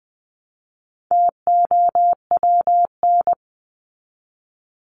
Morse code, heard as a keyed tone, about 20 wpm, 700 Hz. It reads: TOWN
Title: Top 500 Words 20wpm